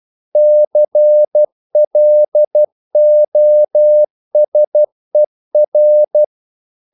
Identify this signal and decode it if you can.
Morse code, heard as a keyed tone, about 12 wpm, 600 Hz. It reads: CLOSER
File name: CW / Practice / EnglishQuotations0112